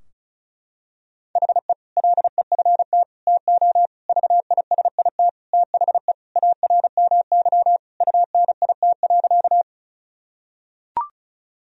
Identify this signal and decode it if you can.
Morse code, heard as a keyed tone, about 36 wpm, 700 Hz. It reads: HE LEFT TO VISIT THE ARMY UNIT.
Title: Sentences from Top 1000 Words - Review Part 2 - 35wpm